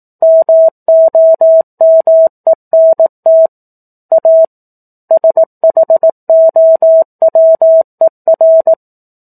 Morse code, heard as a keyed tone, about 18 wpm, 650 Hz. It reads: MOMENT A SHOWER
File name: 04-Chapter-04 18 WPM